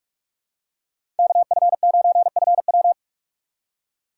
Morse code, heard as a keyed tone, about 44 wpm, 700 Hz. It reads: KF9FW
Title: USA Call Signs - 5 Characters - 3x Character Spacing - 45wpm